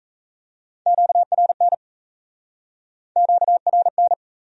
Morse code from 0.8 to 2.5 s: QRN

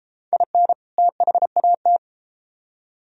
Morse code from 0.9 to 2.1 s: THAT

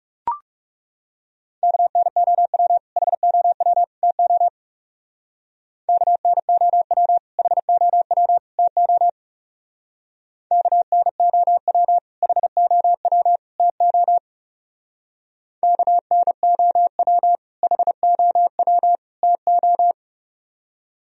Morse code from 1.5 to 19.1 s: KNOW HOW TO KNOW HOW TO KNOW HOW TO KNOW HOW

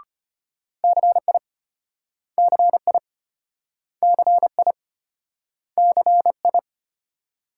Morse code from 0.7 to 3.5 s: CI CI

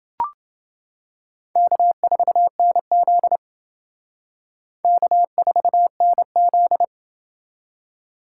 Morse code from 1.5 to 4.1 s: K4NZ

